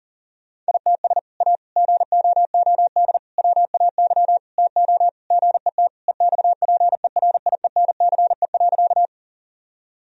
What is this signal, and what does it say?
Morse code, heard as a keyed tone, about 40 wpm, 700 Hz. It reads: ITS A GOOD WAY TO GET EXPERIENCE.
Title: Sentences from Top 1000 Words - Review Part 3 - 40wpm